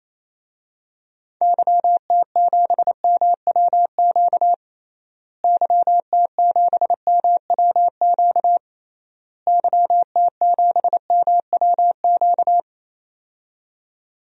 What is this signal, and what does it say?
Morse code, heard as a keyed tone, about 28 wpm, 700 Hz. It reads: YT7MWQ YT7MWQ YT7MWQ
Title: DXCC-100 Call Signs - 6 Characters - Repeated 3x - 28wpm